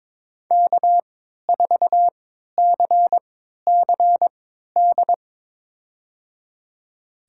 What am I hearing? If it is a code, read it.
Morse code, heard as a keyed tone, about 22 wpm, spaced as for 13 wpm, 700 Hz. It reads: K4CCD
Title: USA Call Signs - 5 Characters - 3x Character Spacing - 22wpm